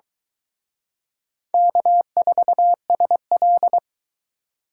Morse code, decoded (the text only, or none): K4SL